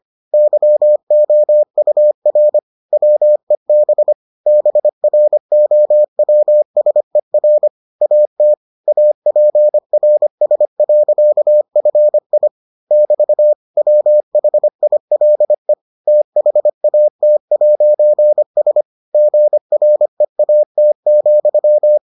Morse code, heard as a keyed tone, about 25 wpm, 600 Hz. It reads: YOUR WEB BROWSER AT APRS.FI <BT> WHILE THAT'S GREAT,